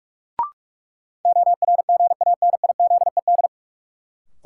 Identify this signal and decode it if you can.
Morse code, heard as a keyed tone, about 46 wpm, 700 Hz. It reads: ORGANIZED
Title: Sentences from Top 1300 Words - New Word Drill - 45wpm